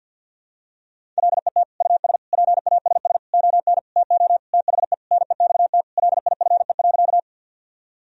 Morse code, heard as a keyed tone, about 48 wpm, 700 Hz. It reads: LET US PASS ON TO THE NEXT LIFE.
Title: Sentences from Top 1000 Words - Review Part 1 - 50wpm